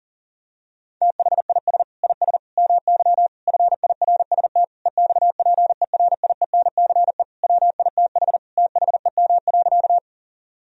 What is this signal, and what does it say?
Morse code, heard as a keyed tone, about 37 wpm, 700 Hz. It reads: THIS IS MY FIRST EXPERIENCE WITH THEM.
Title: Sentences from Top 1000 Words - Review Part 3 - 40wpm